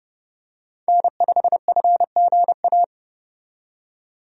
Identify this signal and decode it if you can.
Morse code, heard as a keyed tone, about 29 wpm, 700 Hz. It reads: N5FGA
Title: USA Call Signs - 5 Characters - 30wpm